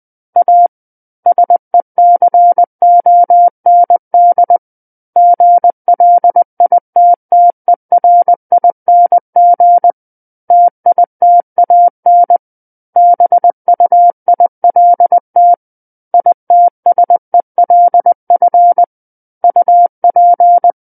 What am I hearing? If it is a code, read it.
Morse code, heard as a keyed tone, about 20 wpm, 700 Hz. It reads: A SECOND GLITTERING TITAN BUILT ITSELF UP
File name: 20 WPM